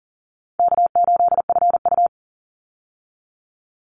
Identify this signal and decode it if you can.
Morse code, heard as a keyed tone, about 40 wpm, 700 Hz. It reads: K8FU
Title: Easy Call Signs Repeated 2x - 40wpm